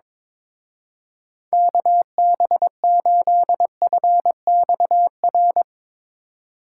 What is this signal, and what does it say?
Morse code, heard as a keyed tone, about 22 wpm, 700 Hz. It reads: KB8FXR
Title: Speed Racing - USA 2x3 Call Signs 22wpm